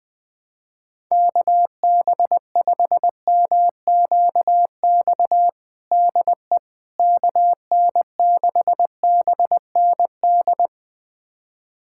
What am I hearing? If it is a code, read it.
Morse code, heard as a keyed tone, about 21 wpm, 700 Hz. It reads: KB5MQX DE KN6BND
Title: USA Callsign DE Callsign 20wpm